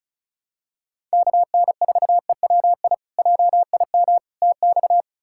Morse code, heard as a keyed tone, about 36 wpm, 700 Hz. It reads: KN4EWI JIM TX
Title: K1USN SST QSO Practice - 35wpm